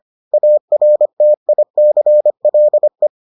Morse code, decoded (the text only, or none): ARTICLE